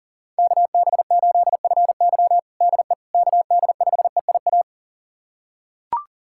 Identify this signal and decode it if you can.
Morse code, heard as a keyed tone, about 40 wpm, 700 Hz. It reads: KB8FY DE KD5EIA
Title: USA Callsign DE Callsign 40wpm